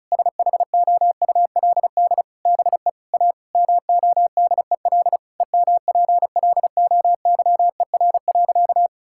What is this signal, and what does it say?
Morse code, heard as a keyed tone, about 36 wpm, 700 Hz. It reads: SHOULD BE A MODEL EMPLOYER.
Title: Sentences from Top 1600 Words 35wpm